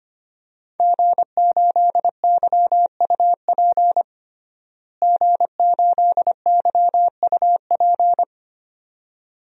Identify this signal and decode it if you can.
Morse code, heard as a keyed tone, about 25 wpm, 700 Hz. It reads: G8YUP G8YUP
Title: DXCC-100 Call Signs - 5 Characters - Repeated 2x - 25wpm